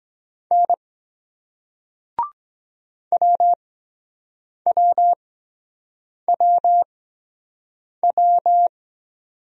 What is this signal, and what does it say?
Morse code, heard as a keyed tone, about 23 wpm, spaced as for 7 wpm, 700 Hz. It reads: N WWWW